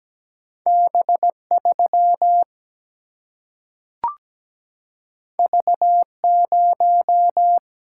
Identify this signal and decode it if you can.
Morse code, heard as a keyed tone, about 17 wpm, 700 Hz. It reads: B 3 V 0
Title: Single-Letter Single-Number 17wpm